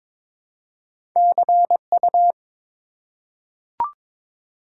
Morse code, heard as a keyed tone, about 22 wpm, 700 Hz. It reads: CU
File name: Top 90 QSO Elements - 4x Character Spacing - 22wpm